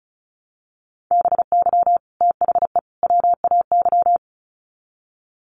Morse code, heard as a keyed tone, about 34 wpm, 700 Hz. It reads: BY THE WAY
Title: Sets of 3 Words 35wpm